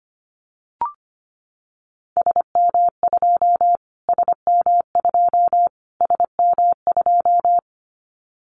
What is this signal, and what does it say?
Morse code, heard as a keyed tone, about 25 wpm, 700 Hz. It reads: SM2 SM2 SM2
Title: Callsign Prefixes for Top 25 DXCC - Repeated 3x - 25wpm